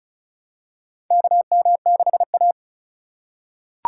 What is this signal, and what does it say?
Morse code, heard as a keyed tone, about 36 wpm, 700 Hz. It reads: KM6A
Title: USA Call Signs - 2x Character Spacing - 35wpm